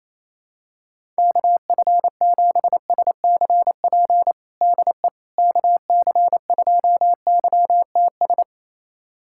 Morse code, heard as a keyed tone, about 28 wpm, 700 Hz. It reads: KF7SCP DE KC2YTS
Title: USA Callsign DE Callsign - 6 Characters - 28wpm